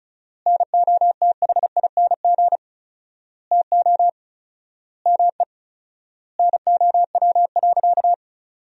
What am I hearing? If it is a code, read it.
Morse code, heard as a keyed tone, about 34 wpm, 700 Hz. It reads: NOTHING TO ME NOW.
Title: 4x Word-Spacing - Sentences from Top 500 Words - 35wpm